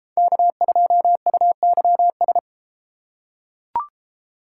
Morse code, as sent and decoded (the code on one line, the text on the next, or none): -.- ..--- ..- -.-- ...
K2UYS